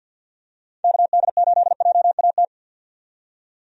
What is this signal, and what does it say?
Morse code, heard as a keyed tone, about 50 wpm, 700 Hz. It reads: KD8JAT
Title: USA Call Signs - 6 Characters - 50wpm